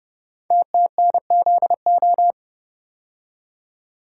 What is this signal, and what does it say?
Morse code, heard as a keyed tone, about 30 wpm, 700 Hz. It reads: TTNZO
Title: Code Groups - 5 Letters - 30wpm